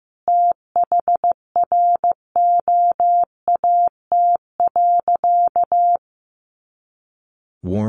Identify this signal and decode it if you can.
Morse code, heard as a keyed tone, about 15 wpm, 700 Hz. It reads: THROAT.
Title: A Sound Of Thunder 15wpm